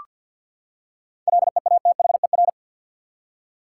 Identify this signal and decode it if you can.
Morse code, heard as a keyed tone, about 46 wpm, 700 Hz. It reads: LEATHER
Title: Seven Letter Words - 50wpm